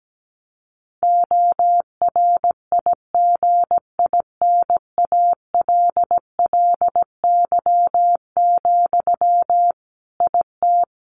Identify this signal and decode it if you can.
Morse code, heard as a keyed tone, about 17 wpm, 700 Hz. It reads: ORIGINALLY, IT
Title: Sentences from Top 1500 Words 17wpm